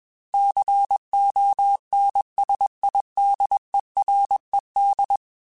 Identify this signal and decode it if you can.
Morse code, heard as a keyed tone, about 21 wpm, 800 Hz. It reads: CONSIDERED
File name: Technical Subjects / Technical Subjects-mp3s / XC @20